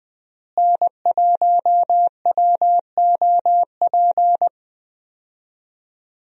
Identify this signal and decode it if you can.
Morse code, heard as a keyed tone, about 20 wpm, 700 Hz. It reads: N1WOP
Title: USA Call Signs - 5 Characters - Repeated 3x - 20wpm